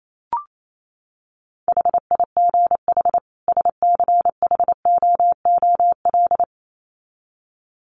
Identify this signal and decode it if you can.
Morse code, heard as a keyed tone, about 28 wpm, 700 Hz. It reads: HIGH SCHOOL
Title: Sets of 2 Words 28wpm